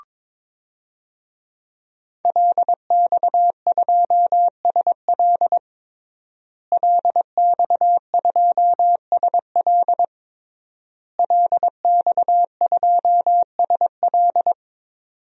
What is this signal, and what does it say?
Morse code, heard as a keyed tone, about 22 wpm, 700 Hz. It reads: LX2SL LX2SL LX2SL
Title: DXCC-100 Call Signs - 5 Characters - Repeated 3x - 22wpm